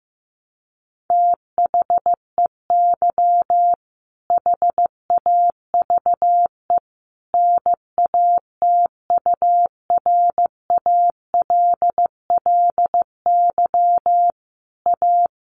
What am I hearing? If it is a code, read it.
Morse code, heard as a keyed tone, about 15 wpm, 700 Hz. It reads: THEY HAVE NATURALLY A
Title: Sentences from Top 1300 Words 15wpm